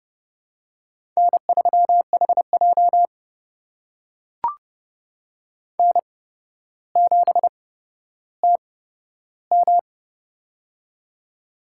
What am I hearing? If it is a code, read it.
Morse code, heard as a keyed tone, about 30 wpm, spaced as for 13 wpm, 700 Hz. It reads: N3HJ N7TM
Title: USA Call Signs - 4 Characters - 8x Character Spacing - 30wpm